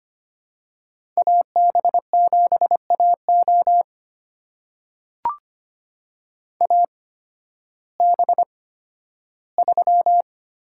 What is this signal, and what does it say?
Morse code, heard as a keyed tone, about 25 wpm, spaced as for 13 wpm, 700 Hz. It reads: AB7AO AB3
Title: USA Call Signs - 5 Characters - 8x Character Spacing - 25wpm